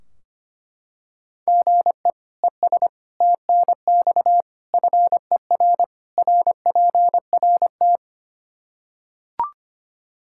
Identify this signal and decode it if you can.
Morse code, heard as a keyed tone, about 25 wpm, 700 Hz. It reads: GE ES TNX FER RPRT
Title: Practice QSO - Exchange 1 - Beginning - 25wpm